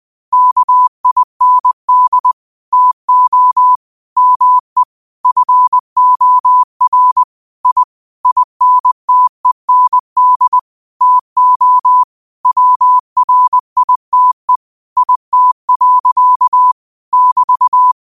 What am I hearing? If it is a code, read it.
Morse code, heard as a keyed tone, about 20 wpm, 1000 Hz. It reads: KIND TO ME FOR I INTEND TO WRITE IT. <BT>